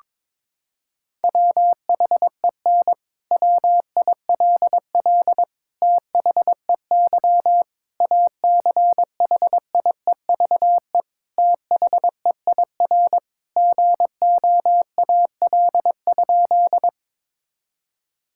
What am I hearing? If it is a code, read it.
Morse code, heard as a keyed tone, about 22 wpm, 700 Hz. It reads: WHEN WILL THEY ACHIEVE THEIR GOAL?
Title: Sentences from Top 1000 Words - Review Part 6 - 22wpm